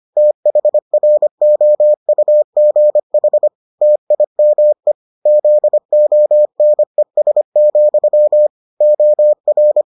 Morse code, heard as a keyed tone, about 25 wpm, 600 Hz. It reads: THROUGH TIME ZONES, OR